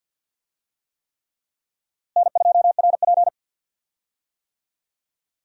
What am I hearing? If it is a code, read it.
Morse code, heard as a keyed tone, about 50 wpm, 700 Hz. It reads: N2RP
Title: USA Call Signs - 4 Characters - Repeated 3x - 50wpm